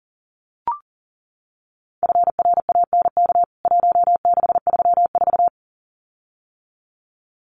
Morse code, heard as a keyed tone, about 40 wpm, 700 Hz. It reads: FRANK 1634
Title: CWT Exchange 40wpm